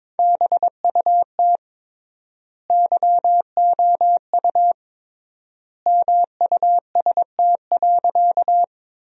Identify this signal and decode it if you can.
Morse code, heard as a keyed tone, about 21 wpm, 700 Hz. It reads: BUT YOU MUST.
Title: 3x Word-Spacing - Sentences from Top 200 Words - 22wpm